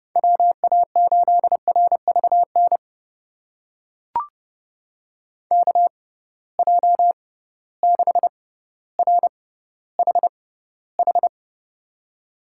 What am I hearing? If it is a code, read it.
Morse code, heard as a keyed tone, about 30 wpm, spaced as for 15 wpm, 700 Hz. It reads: WA8RVN KJ6RHH
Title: USA Call Signs - 6 Characters - 6x Character Spacing - 30wpm